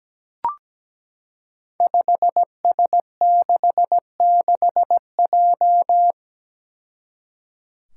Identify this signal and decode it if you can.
Morse code, heard as a keyed tone, about 17 wpm, 700 Hz. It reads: 5S66J
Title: Five Challenging Letter-Number Combinations 17wpm